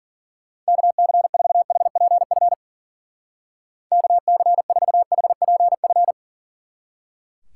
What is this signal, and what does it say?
Morse code, heard as a keyed tone, about 49 wpm, 700 Hz. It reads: KC4HPF KC4HPF
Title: Speed Racing - USA Call Signs 40wpm